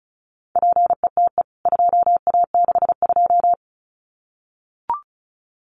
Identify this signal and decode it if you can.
Morse code, heard as a keyed tone, about 34 wpm, 700 Hz. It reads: PETE 2A62
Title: CWT Exchange 35wpm